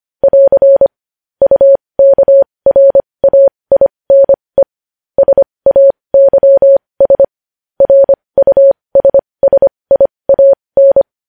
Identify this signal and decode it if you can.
Morse code, heard as a keyed tone, about 25 wpm, spaced as for 20 wpm, 550 Hz. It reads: <AR> UKRAINE SAYS RUSSIAN
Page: news-headlines-in-morse-code-at-20-wpm